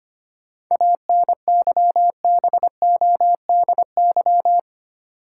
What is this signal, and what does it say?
Morse code, heard as a keyed tone, about 25 wpm, 700 Hz. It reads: ANYBODY
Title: Sets of 5 Words - New Word Drill - 25wpm